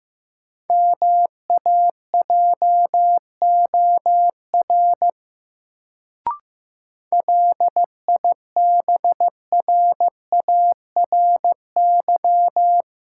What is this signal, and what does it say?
Morse code, heard as a keyed tone, about 15 wpm, 700 Hz. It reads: MAJOR LIBRARY
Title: Sentences from Top 600 Words - New Word Drill - 15wpm